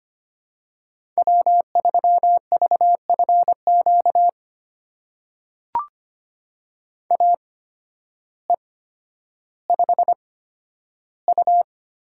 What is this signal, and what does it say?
Morse code, heard as a keyed tone, about 25 wpm, spaced as for 12 wpm, 700 Hz. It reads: W3VFQ AE5U
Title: USA Call Signs - 5 Characters - 8x Character Spacing - 25wpm